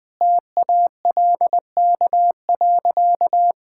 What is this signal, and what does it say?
Morse code, heard as a keyed tone, about 20 wpm, 700 Hz. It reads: TALK.
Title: Sentences with 4-Word Phrases - 20wpm